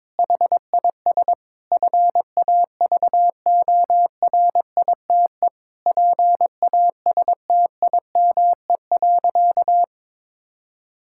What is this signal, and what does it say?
Morse code, heard as a keyed tone, about 22 wpm, 700 Hz. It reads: HIS FAVORITE PASTIME.
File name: Difficult Sentences with Hints - 22wpm